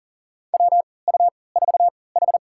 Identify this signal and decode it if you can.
Morse code, heard as a keyed tone, about 40 wpm, spaced as for 23 wpm, 700 Hz. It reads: WU4H
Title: USA Call Signs - 4 Characters - 3x Character Spacing - 40wpm